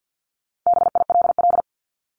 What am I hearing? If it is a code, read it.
Morse code, heard as a keyed tone, about 48 wpm, 700 Hz. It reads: BILL
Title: Top 1501-1750 Words 50wpm